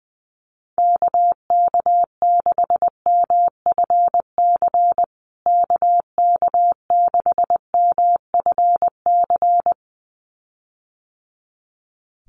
Morse code, heard as a keyed tone, about 20 wpm, 700 Hz. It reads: KK6MFC KK6MFC
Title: USA Call Signs Repeated 2x 20wpm